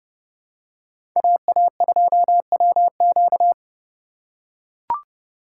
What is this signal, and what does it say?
Morse code, heard as a keyed tone, about 30 wpm, 700 Hz. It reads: AA2WQ